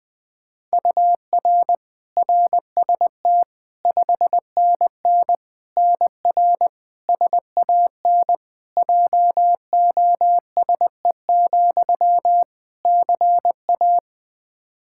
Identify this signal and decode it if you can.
Morse code, heard as a keyed tone, about 20 wpm, 700 Hz. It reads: UR RST 5NN NR SAN JOSE, CA
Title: Practice QSO - Exchange 1 - Beginning - 20wpm